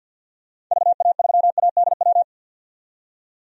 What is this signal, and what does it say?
Morse code, heard as a keyed tone, about 51 wpm, 700 Hz. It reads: VA3ADW
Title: DXCC-100 Call Signs - 6 Characters - 50wpm